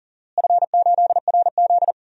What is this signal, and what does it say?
Morse code, heard as a keyed tone, about 40 wpm, 700 Hz. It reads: F8RZ
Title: DXCC-100 Call Signs - 4 Characters - Repeated 3x - 40wpm